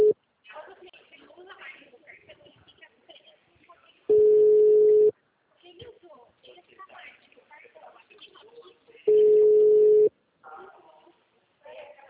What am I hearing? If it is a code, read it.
Ringback tone (Europe, 425 Hz), with speech.
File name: out- -2019- - - .9381